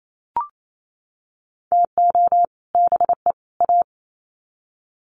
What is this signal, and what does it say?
Morse code, heard as a keyed tone, about 28 wpm, 700 Hz. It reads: TO BE A
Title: Sets of 3 Words 28wpm